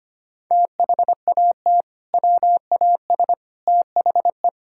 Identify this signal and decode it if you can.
Morse code, heard as a keyed tone, about 25 wpm, 700 Hz. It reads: THAT WAS THE
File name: practice / 25wpm / Sentences with 2-Word Phrases - 25wpm